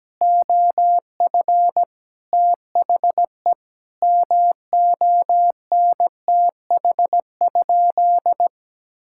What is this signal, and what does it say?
Morse code, heard as a keyed tone, about 17 wpm, 700 Hz. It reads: OF THE MONTH?
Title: Sentences from Top 1000 Words - Review Part 6 - 17wpm